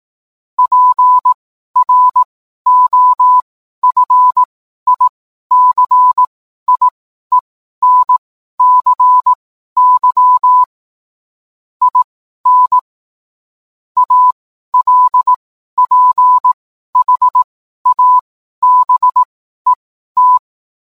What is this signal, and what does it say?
Morse code, heard as a keyed tone, about 18 wpm, spaced as for 12 wpm, 1000 Hz. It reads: PROFICIENCY IN ALPHABET
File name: cwblog / cw mp3 directory / mem morse 12